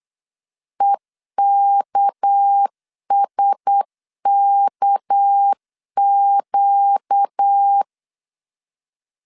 Morse code, heard as a keyed tone, about 8 wpm, 800 Hz. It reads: EKSKQ
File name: LEZIONI / 6 PKX0 / 6 PK 8wpm 3003